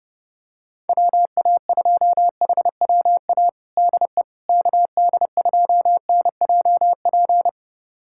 Morse code, heard as a keyed tone, about 30 wpm, 700 Hz. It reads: WA2HWA DE KD2NJP
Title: USA Callsign DE Callsign - 6 Characters - 30wpm